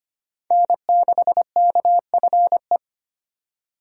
Morse code, heard as a keyed tone, about 25 wpm, 700 Hz. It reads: N6KFE
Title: USA Call Signs - 5 Characters - 25wpm